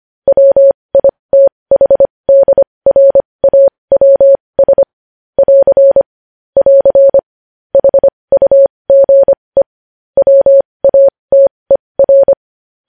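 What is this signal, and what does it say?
Morse code, heard as a keyed tone, about 25 wpm, spaced as for 20 wpm, 550 Hz. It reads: WITHDRAWS <AR> <AR> HUGE WATER
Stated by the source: Morse code news headlines sent at 20.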